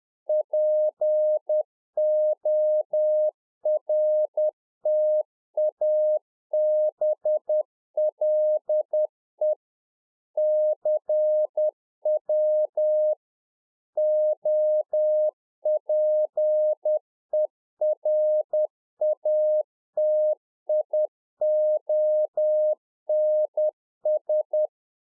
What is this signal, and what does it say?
Morse code, heard as a keyed tone, about 10 wpm, 610 Hz. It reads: PORTABLE CW OPERATIONS
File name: 2011-04-mp3 / RICK JJ0MPI 10WPM